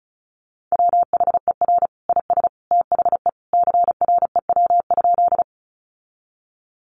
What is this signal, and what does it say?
Morse code, heard as a keyed tone, about 34 wpm, 700 Hz. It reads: WHER IS THE CREW?
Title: Sentences from Top 1500 Words 35wpm